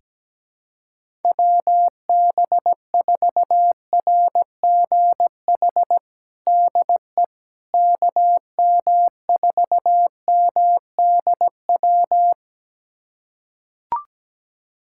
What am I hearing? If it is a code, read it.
Morse code, heard as a keyed tone, about 17 wpm, 700 Hz. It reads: WB4RGH DE KM4MDW
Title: USA Callsign DE Callsign 17wpm